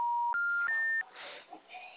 Special information tone, with noise.
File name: out- -2007- - - .554